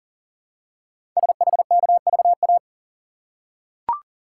Morse code, heard as a keyed tone, about 40 wpm, 700 Hz. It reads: SHKVA
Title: Code Groups - 5 Letters - 40wpm